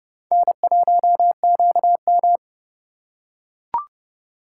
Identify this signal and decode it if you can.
Morse code, heard as a keyed tone, about 28 wpm, 700 Hz. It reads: N1QM